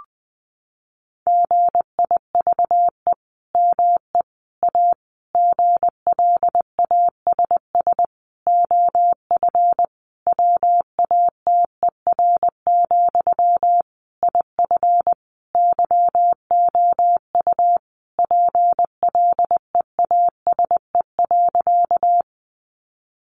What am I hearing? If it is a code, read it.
Morse code, heard as a keyed tone, about 20 wpm, 700 Hz. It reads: GIVE ME A GLASS OF WATER, IF YOU PLEASE.
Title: Sentences from Top 1500 Words 20wpm